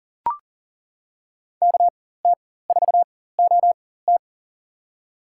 Morse code, heard as a keyed tone, about 40 wpm, spaced as for 19 wpm, 700 Hz. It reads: KT4OT
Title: USA Call Signs - 4x Character Spacing - 40wpm